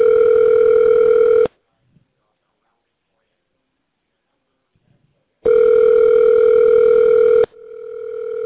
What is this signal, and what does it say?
Ringback tone (North America), 440 + 480 Hz.